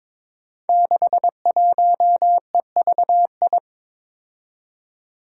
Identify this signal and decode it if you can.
Morse code, heard as a keyed tone, about 23 wpm, 700 Hz. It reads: 61EVI
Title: Five Challenging Letter-Number Combinations 22wpm